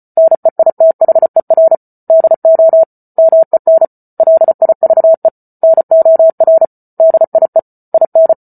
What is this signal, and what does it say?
Morse code, heard as a keyed tone, about 34 wpm, 650 Hz. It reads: NEITHER DO MEN LIVE NOR DIE IN
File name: Book Two / 08-Chapter 8 - 34 WPM